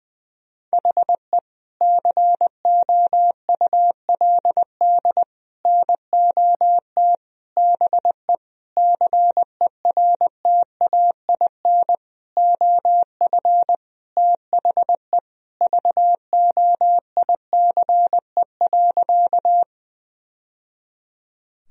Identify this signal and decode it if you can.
Morse code, heard as a keyed tone, about 21 wpm, 700 Hz. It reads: HE COULD NOT BE CERTAIN OF THE VOICE.
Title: Sentences from Top 1000 Words - Review Part 2 - 20wpm